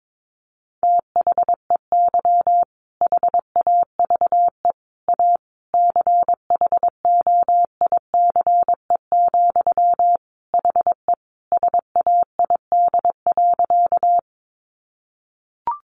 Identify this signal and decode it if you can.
Morse code, heard as a keyed tone, about 22 wpm, 700 Hz. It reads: THEY HAVE A CHOICE, HE SAID.
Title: Sentences from Top 900 Words 22wpm